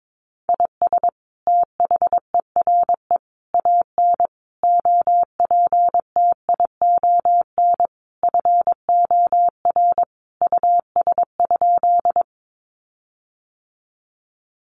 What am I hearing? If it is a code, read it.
Morse code, heard as a keyed tone, about 22 wpm, 700 Hz. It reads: IS THERE AN OPTION FOR US?
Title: Sentences from Top 1300 Words 22wpm